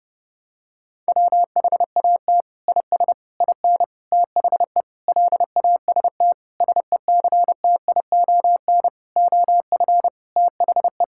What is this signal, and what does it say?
Morse code, heard as a keyed tone, about 30 wpm, 700 Hz. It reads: WHAT IS IN THE LAST SECTION OF THE
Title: Sentences from Top 1000 Words - Review Part 5 - 30wpm